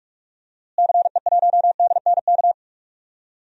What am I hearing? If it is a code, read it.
Morse code, heard as a keyed tone, about 45 wpm, 700 Hz. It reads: CE1DNK
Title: DXCC-100 Call Signs - 6 Characters - Repeated 2x - 45wpm